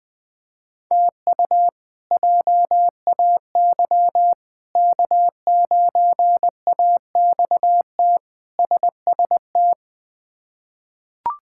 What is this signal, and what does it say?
Morse code, heard as a keyed tone, about 20 wpm, 700 Hz. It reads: TU JAY K9AXT SST
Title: K1USN SST QSO Practice - 20wpm